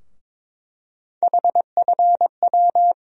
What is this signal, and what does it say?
Morse code, heard as a keyed tone, about 22 wpm, 700 Hz. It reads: HFW
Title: USA Call Sign Postfixes - 3 Letters 22wpm